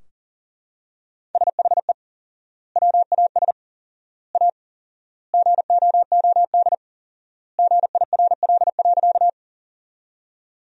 Morse code, heard as a keyed tone, about 40 wpm, 700 Hz. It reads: SHE WAS A GOOD GIRL.
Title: 4x Word-Spacing - Sentences from Top 400 Words - 40wpm